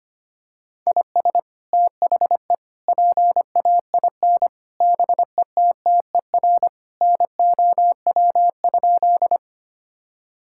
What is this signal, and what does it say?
Morse code, heard as a keyed tone, about 25 wpm, 700 Hz. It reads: IS THE PAIN BETTER NOW?
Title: Sentences from Top 1600 Words 25wpm